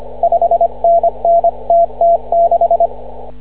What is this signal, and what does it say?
Morse code, heard as a keyed tone, about 25 wpm, 680 Hz. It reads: 5NN TT6
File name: CW / ha6oi 2